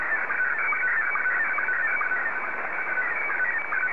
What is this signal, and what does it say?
SSTV (slow-scan television), Scottie 1.
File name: sstv / S1-JA3QJQ